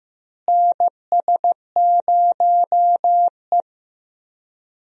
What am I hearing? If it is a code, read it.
Morse code, heard as a keyed tone, about 15 wpm, 700 Hz. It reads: NS0E
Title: USA Call Signs - 4 Characters - Repeated 2x - 15wpm